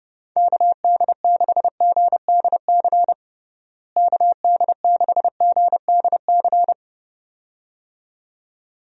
Morse code, heard as a keyed tone, about 32 wpm, 700 Hz. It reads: KD6GDC KD6GDC
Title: USA Call Signs - 6 Characters - Repeated 2x - 30wpm